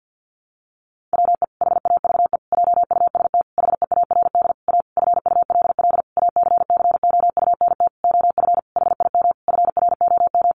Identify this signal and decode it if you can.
Morse code, heard as a keyed tone, about 50 wpm, 700 Hz. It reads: WE HAVE JUST HEARD A FULL ACCOUNT OF HIM FROM
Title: Sentences from Top 900 Words 50wpm